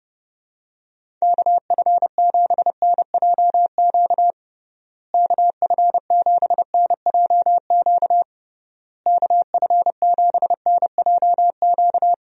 Morse code, heard as a keyed tone, about 32 wpm, 700 Hz. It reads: KF7NJQ KF7NJQ KF7NJQ
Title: USA Call Signs - 6 Characters - Repeated 3x - 30wpm